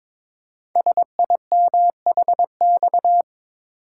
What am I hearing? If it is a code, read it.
Morse code, heard as a keyed tone, about 22 wpm, 700 Hz. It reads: SIMHX
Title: Code Groups - 5 Letters - 22wpm